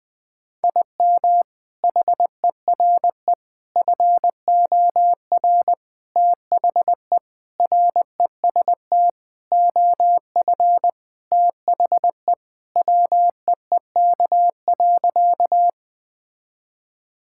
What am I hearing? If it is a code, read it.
Morse code, heard as a keyed tone, about 20 wpm, 700 Hz. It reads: IM HERE FOR THE REST OF THE WEEK.
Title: Sentences with 4-Word Phrases - 20wpm